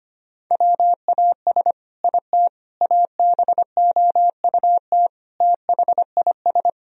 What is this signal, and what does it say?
Morse code, heard as a keyed tone, about 25 wpm, 700 Hz. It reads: WAS IT ABOUT THIS
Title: Sentences from Top 1000 Words - Review Part 3 - 25wpm